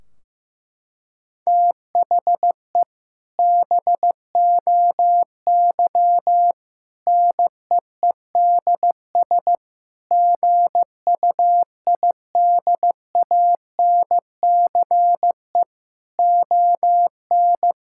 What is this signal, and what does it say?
Morse code, heard as a keyed tone, about 16 wpm, 700 Hz. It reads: THE BOY NEEDS GUIDANCE ON
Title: Sentences from Top 1600 Words 15wpm